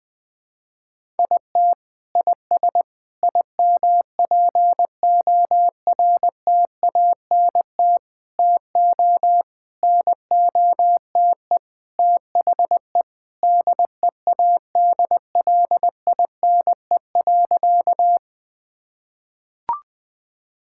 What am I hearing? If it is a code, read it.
Morse code, heard as a keyed tone, about 20 wpm, 700 Hz. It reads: IT IS IMPORTANT TO NOTE THE DEADLINE.
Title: Sentences with 5-Word Phrases - 20wpm